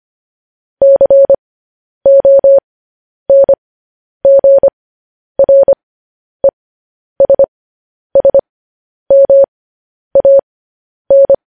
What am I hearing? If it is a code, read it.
Morse code, heard as a keyed tone, about 25 wpm, spaced as for 10 wpm, 550 Hz. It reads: CONGRESSMAN
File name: podcast / News headlines-1x-25x10